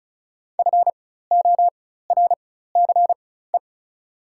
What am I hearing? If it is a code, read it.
Morse code, heard as a keyed tone, about 36 wpm, spaced as for 16 wpm, 690 Hz. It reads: FORCE
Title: 4x Character Spacing - Top 500 Words - 35wpm